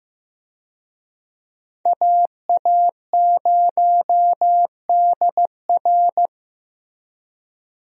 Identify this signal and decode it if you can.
Morse code, heard as a keyed tone, about 15 wpm, 700 Hz. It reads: AA0DR
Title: USA Call Signs - 5 Characters - Repeated 2x - 15wpm